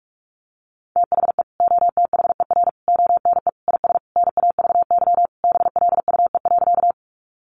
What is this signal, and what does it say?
Morse code, heard as a keyed tone, about 44 wpm, 700 Hz. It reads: THE OTHER ONE IS NAVY BLUE.
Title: Additional Easy Phrases 45wpm